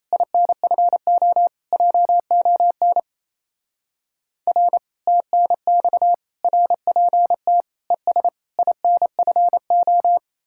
Morse code, heard as a keyed tone, about 34 wpm, 700 Hz. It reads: INFO JON R TNX RPT ES INFO
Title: Speed-Racing Common QSO Phrases - 28wpm